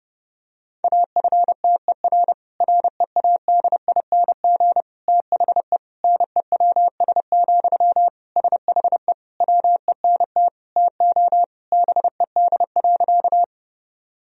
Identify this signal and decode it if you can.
Morse code, heard as a keyed tone, about 30 wpm, 700 Hz. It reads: AFTER READING THE NEWS, SHE WENT TO BED.
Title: Sentences from Top 1000 Words - Review Part 3 - 30wpm